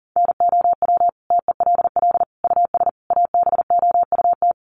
Morse code, heard as a keyed tone, about 40 wpm, 700 Hz. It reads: NOW TELL US ABOUT
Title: Sentences from Top 600 Words 40wpm